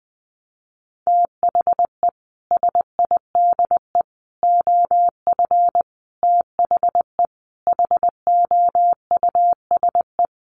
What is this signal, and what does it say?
Morse code, heard as a keyed tone, about 20 wpm, 700 Hz. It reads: THE SIDE OF THE HOUSE
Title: Sets of 5 Words Encore2 20wpm